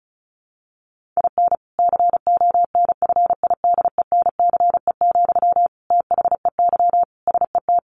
Morse code, heard as a keyed tone, about 34 wpm, 700 Hz. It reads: IN CONFIDENCE, THEY SET
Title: Sentences from Top 1500 Words 35wpm